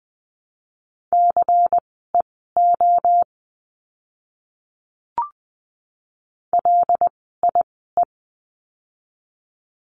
Morse code, heard as a keyed tone, about 20 wpm, spaced as for 14 wpm, 700 Hz. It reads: CEO LIE
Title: 094 Lesson 16f - Letter c - Words 20wpm